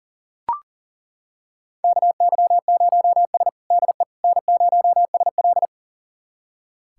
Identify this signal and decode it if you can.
Morse code, heard as a keyed tone, about 40 wpm, 700 Hz. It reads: KY0S DE N0SL
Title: USA Callsign DE Callsign - 4 Characters - 40wpm